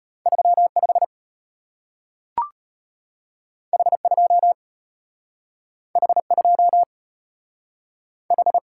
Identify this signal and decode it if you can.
Morse code, heard as a keyed tone, about 33 wpm, 700 Hz. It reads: 35 H2 H2 H